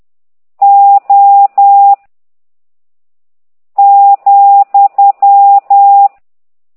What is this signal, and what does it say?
Morse code, heard as a keyed tone, about 10 wpm, 800 Hz. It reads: O ,